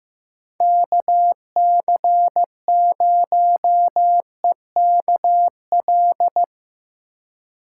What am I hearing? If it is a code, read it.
Morse code, heard as a keyed tone, about 15 wpm, 700 Hz. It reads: KC0EKL
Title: USA Call Signs - 6 Characters - 2x Character Spacing - 15wpm